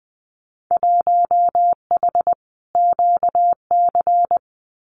Morse code, heard as a keyed tone, about 20 wpm, 700 Hz. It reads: 1H QC
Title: Winter Field Day Exchanges 20wpm